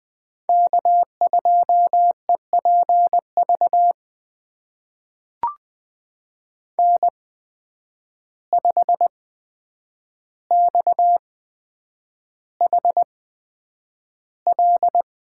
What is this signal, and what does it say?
Morse code, heard as a keyed tone, about 20 wpm, spaced as for 9 wpm, 700 Hz. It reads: K2EPV N5XHL
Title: USA Call Signs - 8x Character Spacing - 20wpm